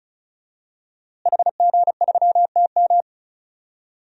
Morse code, heard as a keyed tone, about 34 wpm, 700 Hz. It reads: HG3TM
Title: DXCC-100 Call Signs - 5 Characters - Repeated 2x - 35wpm